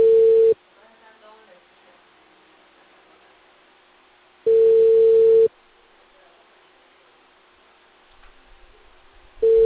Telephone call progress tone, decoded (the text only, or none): ringback tone